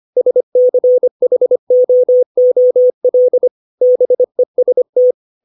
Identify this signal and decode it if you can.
Morse code, heard as a keyed tone, about 25 wpm, 500 Hz. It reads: SCHOOL BEST
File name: 632 words / 632 Words 25WPM